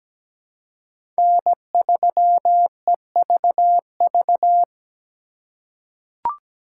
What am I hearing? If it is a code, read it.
Morse code, heard as a keyed tone, about 17 wpm, 700 Hz. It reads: N3EVV
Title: USA Call Signs - 5 Characters - 4x Character Spacing - 17wpm